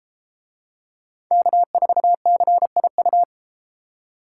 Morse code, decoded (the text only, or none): K4CIU